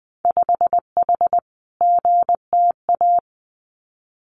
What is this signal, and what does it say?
Morse code, heard as a keyed tone, about 20 wpm, 700 Hz. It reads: 5H GTA
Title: Winter Field Day Exchanges Encore 20wpm